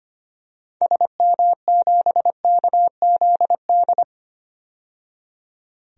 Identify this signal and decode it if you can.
Morse code, heard as a keyed tone, about 25 wpm, 700 Hz. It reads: SM7KZD
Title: DXCC-100 Call Signs - 6 Characters - 25wpm